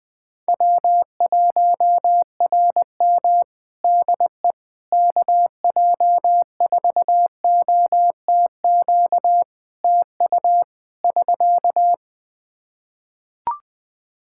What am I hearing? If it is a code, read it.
Morse code, heard as a keyed tone, about 20 wpm, 700 Hz. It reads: W1RM DE KJ4OTQ TU <SK>
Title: Practice QSO - Exchange 4 - Ending - 20wpm